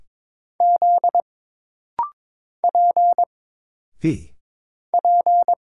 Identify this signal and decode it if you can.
Morse code, heard as a keyed tone, about 23 wpm, 700 Hz. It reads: Z P P
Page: This rapid-fire timing is geared to push your ICR (Instant Character Recognition) skills.